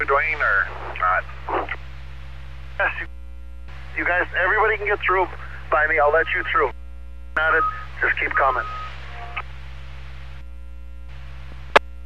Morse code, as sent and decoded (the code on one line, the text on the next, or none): none